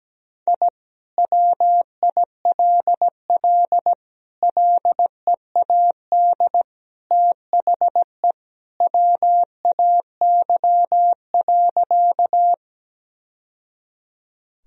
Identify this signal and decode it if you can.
Morse code, heard as a keyed tone, about 17 wpm, 700 Hz. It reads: I WILL LEAD THE WAY.
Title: Sentences from Top 1000 Words - Review Part 2 - 17wpm